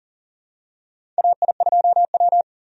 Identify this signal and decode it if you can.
Morse code, heard as a keyed tone, about 40 wpm, 700 Hz. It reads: AI2W
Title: USA Call Signs - 4 Characters - 40wpm